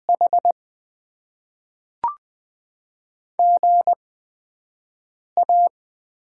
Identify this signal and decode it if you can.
Morse code, heard as a keyed tone, about 20 wpm, 700 Hz. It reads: H G A
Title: Top 20 QSO Elements - 8x Character Spacing - 20wpm